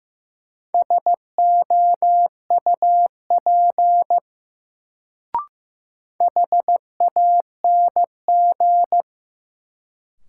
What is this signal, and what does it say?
Morse code, heard as a keyed tone, about 15 wpm, 700 Hz. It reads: SOUP HANG
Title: Sentences from Top 1600 Words - New Word Drill Alt - 15wpm